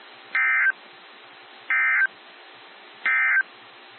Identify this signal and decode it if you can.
Emergency Alert System (SAME) tones, heard as harsh buzzing bursts. It is heard from 0.2 to 3.5 s.
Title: NOAA Weather Radio - WXJ73